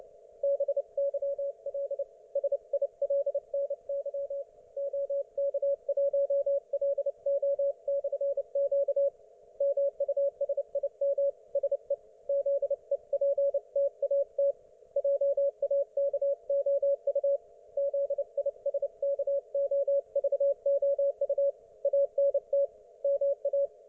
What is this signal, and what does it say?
Morse code, heard as a keyed tone, about 29 wpm, 560 Hz. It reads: BYL SILNY OK1LO/Q MUSIM SE ZEPTAT JAKOU ZISKOVOU ANT MA